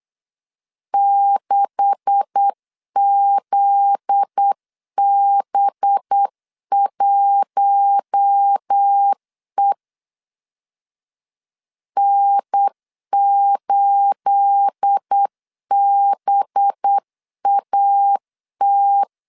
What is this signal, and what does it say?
Morse code, heard as a keyed tone, about 8 wpm, 800 Hz. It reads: 6ZB1E N8BAT
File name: LEZIONI / 9 286 8wpm 3425